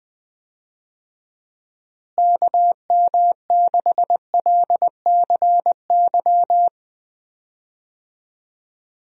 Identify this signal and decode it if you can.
Morse code, heard as a keyed tone, about 20 wpm, 700 Hz. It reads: KM6LCY
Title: USA Call Signs - 6 Characters - Repeated 2x - 20wpm